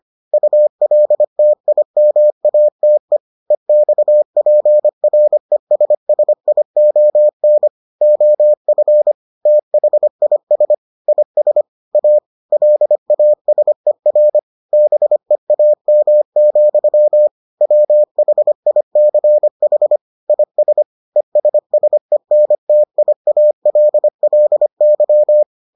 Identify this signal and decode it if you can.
Morse code, heard as a keyed tone, about 25 wpm, 600 Hz. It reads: ULTIMATE EXPRESSION OF THIS IS A LASER BEAM, WHICH IS ESSENTIALLY